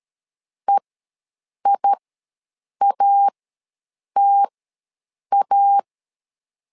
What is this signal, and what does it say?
Morse code, heard as a keyed tone, about 13 wpm, spaced as for 7 wpm, 800 Hz. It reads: EIATA
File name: LEZIONI / 2 UHDN / 2 UHD ws 12wpm 1722